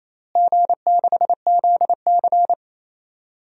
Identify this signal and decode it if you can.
Morse code, heard as a keyed tone, about 28 wpm, 700 Hz. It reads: G6ZC
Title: DXCC-100 Call Signs - 4 Characters - Repeated 2x - 28wpm